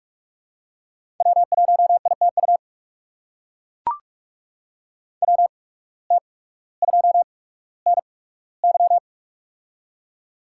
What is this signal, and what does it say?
Morse code, heard as a keyed tone, about 42 wpm, spaced as for 19 wpm, 700 Hz. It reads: W1ITU WT2NY
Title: USA Call Signs - 5 Characters - 8x Character Spacing - 45wpm